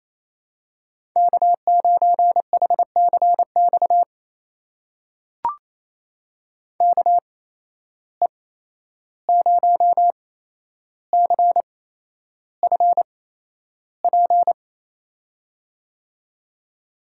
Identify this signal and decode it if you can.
Morse code, heard as a keyed tone, about 28 wpm, spaced as for 11 wpm, 700 Hz. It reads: K9HCX KE0CFP
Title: USA Call Signs - 8x Character Spacing - 28wpm